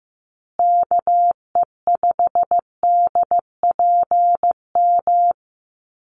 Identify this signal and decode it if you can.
Morse code, heard as a keyed tone, about 15 wpm, 700 Hz. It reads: KE5DPM
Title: USA Call Signs Repeated 2x 15wpm